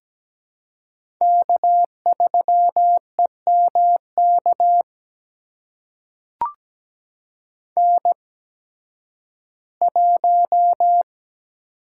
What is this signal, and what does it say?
Morse code, heard as a keyed tone, about 17 wpm, spaced as for 11 wpm, 700 Hz. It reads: K3EMK N1
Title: USA Call Signs - 5 Characters - 8x Character Spacing - 17wpm